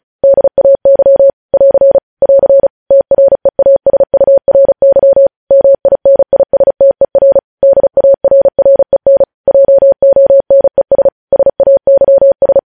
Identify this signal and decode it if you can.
Morse code, heard as a keyed tone, about 36 wpm, 550 Hz. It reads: DAY <AR> <AR> TREASURY MINISTER DARREN JONES SAYS
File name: podcast / News headlines-1x-35x35